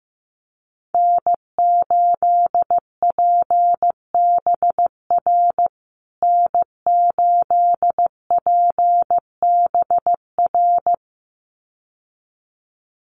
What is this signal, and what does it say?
Morse code, heard as a keyed tone, about 15 wpm, 700 Hz. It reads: N8PBR N8PBR
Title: USA Call Signs Repeated 2x 15wpm